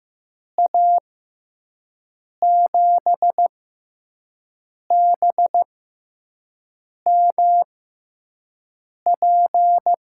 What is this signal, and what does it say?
Morse code, heard as a keyed tone, about 15 wpm, spaced as for 5 wpm, 700 Hz. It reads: A7BMP
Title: USA Call Signs - 6 Characters - 6x Character Spacing - 15wpm